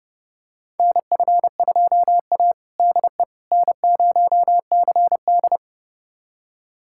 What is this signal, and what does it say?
Morse code, heard as a keyed tone, about 30 wpm, 700 Hz. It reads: NF2A DE N0CD
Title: USA Callsign DE Callsign - 4 Characters - 30wpm